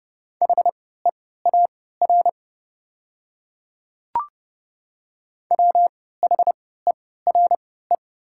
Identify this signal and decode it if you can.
Morse code, heard as a keyed tone, about 30 wpm, spaced as for 17 wpm, 690 Hz. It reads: HEAR WHERE
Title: 3x Character Spacing - Top 300 Words - 30wpm